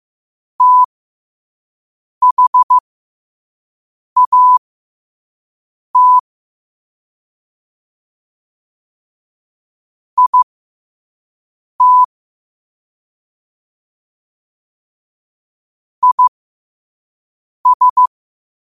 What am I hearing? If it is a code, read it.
Morse code, heard as a keyed tone, about 15 wpm, spaced as for 5 wpm, 1000 Hz. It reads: THAT IT IS
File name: qotdfiles / 5wpm